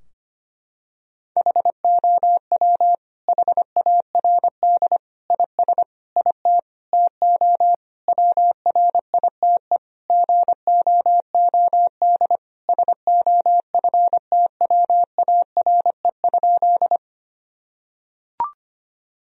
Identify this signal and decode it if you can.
Morse code, heard as a keyed tone, about 25 wpm, 700 Hz. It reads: HOW HARD IS IT TO WRITE GOOD SOFTWARE?
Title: Sentences from Top 1000 Words - Review Part 3 - 25wpm